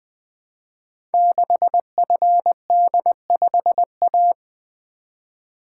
Morse code, heard as a keyed tone, about 20 wpm, 700 Hz. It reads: 6FD5A
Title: Code Groups - 5 Letter:Numbers - 20wpm